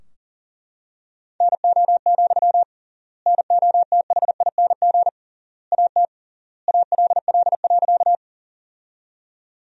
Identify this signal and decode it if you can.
Morse code, heard as a keyed tone, about 40 wpm, 700 Hz. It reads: NO, NOTHING AT ALL.
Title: 3x Word-Spacing - Sentences from Top 500 Words - 40wpm